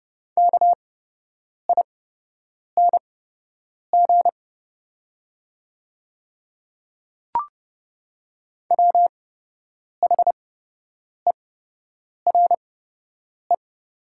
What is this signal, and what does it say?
Morse code, heard as a keyed tone, about 30 wpm, spaced as for 8 wpm, 700 Hz. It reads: KING WHERE
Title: 8x Character Spacing - Top 500 Words - 30wpm